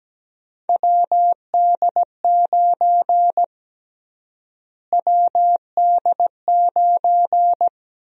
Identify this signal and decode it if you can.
Morse code, heard as a keyed tone, about 17 wpm, 700 Hz. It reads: WD9 WD9
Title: USA Call Sign Prefixes - Repeated 2x - 17wpm